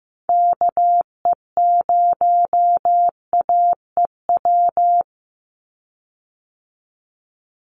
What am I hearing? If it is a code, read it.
Morse code, heard as a keyed tone, about 15 wpm, 700 Hz. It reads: KE0AEW
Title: USA Call Signs 15wpm